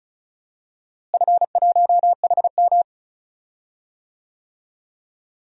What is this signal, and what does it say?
Morse code, heard as a keyed tone, about 34 wpm, 700 Hz. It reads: F1HM
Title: DXCC-100 Call Signs - 4 Characters - Repeated 2x - 35wpm